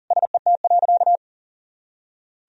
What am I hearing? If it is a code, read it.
Morse code, heard as a keyed tone, about 40 wpm, 700 Hz. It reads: SET.
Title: Sentences from Top 400 Words and Abbreviations 40wpm